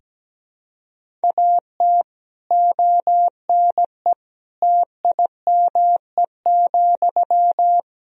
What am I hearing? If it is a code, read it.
Morse code, heard as a keyed tone, about 17 wpm, 700 Hz. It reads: AT ONE TIME,
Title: Sentences from Top 1000 Words - Review Part 1 - 17wpm